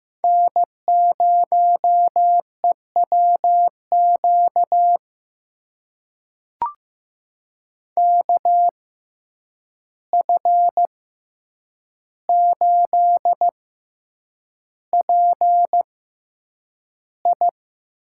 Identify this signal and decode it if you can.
Morse code, heard as a keyed tone, about 15 wpm, spaced as for 8 wpm, 700 Hz. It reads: N0EWQ KF8PI
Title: USA Call Signs - 5 Characters - 6x Character Spacing - 15wpm